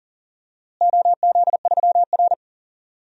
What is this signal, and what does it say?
Morse code, heard as a keyed tone, about 40 wpm, 700 Hz. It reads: OZ3R
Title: DXCC-100 Call Signs - 4 Characters - Repeated 2x - 40wpm